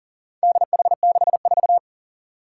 Morse code, heard as a keyed tone, about 40 wpm, 690 Hz. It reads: DH64
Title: Four Challenging Letter-Number Combinations 40wpm